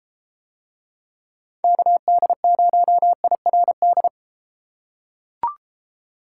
Morse code, heard as a keyed tone, about 33 wpm, 700 Hz. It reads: KD0IRD